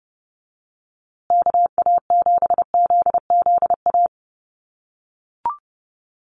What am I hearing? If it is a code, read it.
Morse code, heard as a keyed tone, about 30 wpm, 700 Hz. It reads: KA7ZZA
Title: USA Call Signs Repeated 3x 30wpm